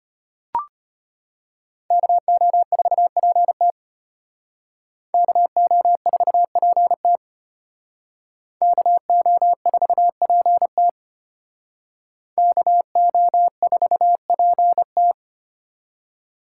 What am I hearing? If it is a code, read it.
Morse code, heard as a keyed tone, about 32 wpm, 700 Hz. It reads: KO4PT KO4PT KO4PT KO4PT